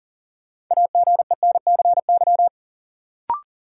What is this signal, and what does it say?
Morse code, heard as a keyed tone, about 40 wpm, 700 Hz. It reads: AGENCY
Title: Sentences from Top 1300 Words - New Word Drill - 40wpm